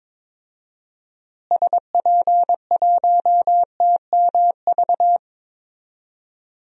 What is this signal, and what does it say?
Morse code, heard as a keyed tone, about 21 wpm, 700 Hz. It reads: SP1TMV
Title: DXCC-100 Call Signs - 6 Characters - Repeated 2x - 22wpm